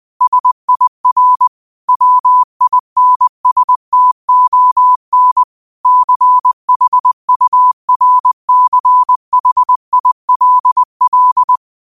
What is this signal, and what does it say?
Morse code, heard as a keyed tone, about 20 wpm, 1000 Hz. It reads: SIR WINSTON CHURCHILL